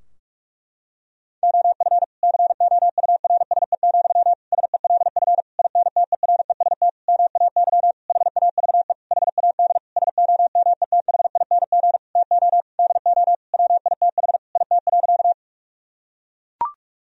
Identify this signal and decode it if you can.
Morse code, heard as a keyed tone, about 41 wpm, 700 Hz. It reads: OF COURSE, SELF INTEREST MAY HAVE HAD SOMETHING TO DO WITH IT.
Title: Sentences from Top 1000 Words - Review Part 2 - 45wpm